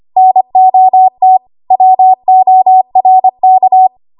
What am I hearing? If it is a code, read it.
Morse code, heard as a keyed tone, about 25 wpm, 750 Hz. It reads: NOT WORK
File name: Archive / 25 WPM / 25WPM